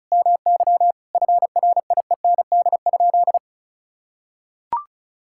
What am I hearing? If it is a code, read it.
Morse code, heard as a keyed tone, about 36 wpm, 700 Hz. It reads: MY FRIEND?
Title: Sentences from Top 1600 Words 35wpm